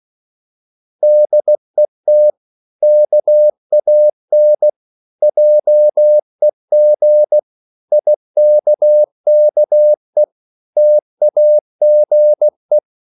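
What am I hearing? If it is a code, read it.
Morse code, heard as a keyed tone, about 16 wpm, 600 Hz. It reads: DET KAN JEG IKKE TAGE
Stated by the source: Slow (QRS) Morse code at 16 wpm, from CW med Gnister.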